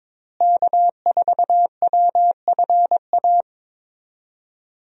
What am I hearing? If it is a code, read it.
Morse code, heard as a keyed tone, about 22 wpm, 700 Hz. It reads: K4WFA
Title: USA Call Signs - 5 Characters - Repeated 2x - 22wpm